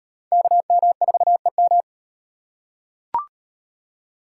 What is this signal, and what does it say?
Morse code, heard as a keyed tone, about 38 wpm, 700 Hz. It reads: KM4EM